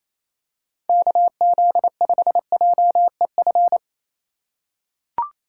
Morse code, heard as a keyed tone, about 28 wpm, 700 Hz. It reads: KZ5JEF
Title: USA Call Signs - 6 Characters - 4x Character Spacing - 28wpm